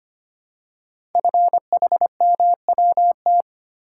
Morse code, heard as a keyed tone, about 25 wpm, 700 Hz. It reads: FHMWT
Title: Code Groups - 5 Letters - 25wpm